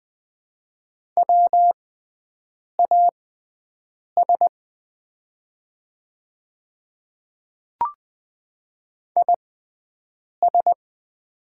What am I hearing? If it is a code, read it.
Morse code, heard as a keyed tone, about 20 wpm, spaced as for 7 wpm, 700 Hz. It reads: WAS IS
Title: 6x Character Spacing - Top 100 Words - 20wpm